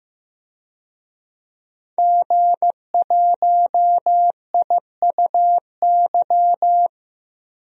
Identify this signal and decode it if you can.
Morse code, heard as a keyed tone, about 15 wpm, 700 Hz. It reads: G1IUY
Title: DXCC-100 Call Signs - 5 Characters - 15wpm